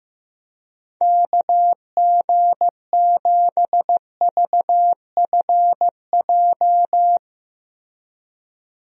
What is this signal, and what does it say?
Morse code, heard as a keyed tone, about 15 wpm, 700 Hz. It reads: KG7VFJ
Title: USA Call Signs - 6 Characters - Repeated 3x - 15wpm